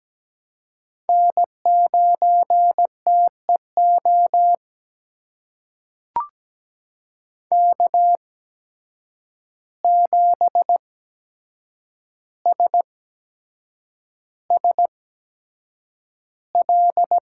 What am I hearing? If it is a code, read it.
Morse code, heard as a keyed tone, about 17 wpm, spaced as for 7 wpm, 700 Hz. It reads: N9TEO K7SSL
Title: USA Call Signs - 5 Characters - 8x Character Spacing - 17wpm